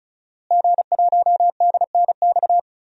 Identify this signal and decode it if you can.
Morse code, heard as a keyed tone, about 36 wpm, 700 Hz. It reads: G1DNX
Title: DXCC-100 Call Signs - 5 Characters - Repeated 3x - 35wpm